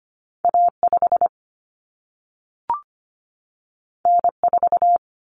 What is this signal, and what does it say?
Morse code, heard as a keyed tone, about 25 wpm, 700 Hz. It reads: A5 N4
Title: Common Cut Numbers - 2-Digit 25wpm